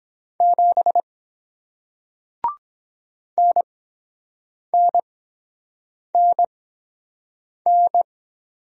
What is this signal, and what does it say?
Morse code, heard as a keyed tone, about 22 wpm, spaced as for 7 wpm, 700 Hz. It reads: 7 NNNN